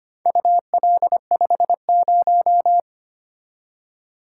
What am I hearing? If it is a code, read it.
Morse code, heard as a keyed tone, about 25 wpm, 700 Hz. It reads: UL50
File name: practice / 25wpm / Four Letter-Number Combinations - 25wpm